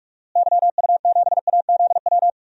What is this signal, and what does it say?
Morse code, heard as a keyed tone, about 46 wpm, 700 Hz. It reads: YU7AZW
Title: DXCC-100 Call Signs - 6 Characters - Repeated 3x - 45wpm